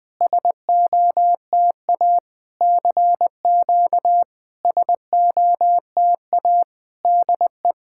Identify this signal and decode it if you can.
Morse code, heard as a keyed tone, about 21 wpm, 700 Hz. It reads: SOTA CQ SOTA DE
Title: SOTA QSO Practice 20wpm